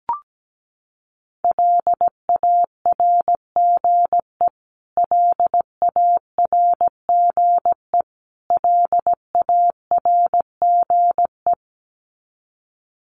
Morse code, heard as a keyed tone, about 17 wpm, 700 Hz. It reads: LARGE LARGE LARGE
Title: Top 200 Words Repeated 17wpm